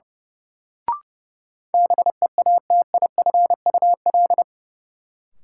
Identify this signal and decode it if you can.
Morse code, heard as a keyed tone, about 30 wpm, 700 Hz. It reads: BEATIFUL
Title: Sentences from Top 1300 Words - New Word Drill Alt - 30wpm